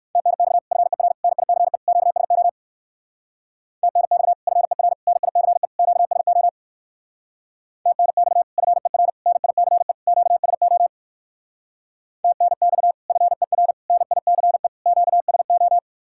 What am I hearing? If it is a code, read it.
Morse code, heard as a keyed tone, about 61 wpm, 700 Hz. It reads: TNX FER NICE QSO TNX FER NICE QSO TNX FER NICE QSO TNX FER NICE QSO